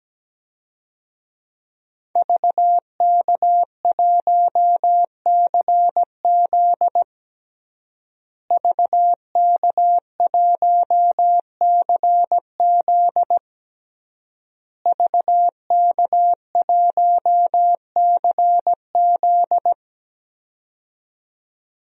Morse code, heard as a keyed tone, about 17 wpm, 700 Hz. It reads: VK1CZ VK1CZ VK1CZ
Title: DXCC-100 Call Signs - 5 Characters - Repeated 3x - 17wpm